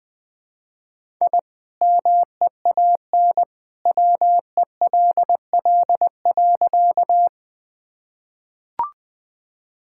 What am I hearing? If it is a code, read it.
Morse code, heard as a keyed tone, about 20 wpm, 700 Hz. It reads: I MEAN WELL.
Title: Sentences with 2-Word Phrases - 20wpm